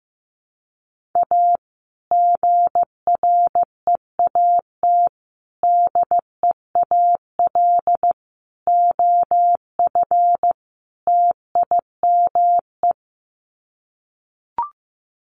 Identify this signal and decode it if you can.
Morse code, heard as a keyed tone, about 15 wpm, 700 Hz. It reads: A GREAT DEAL OF TIME
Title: Sets of 5 Words Encore2 15wpm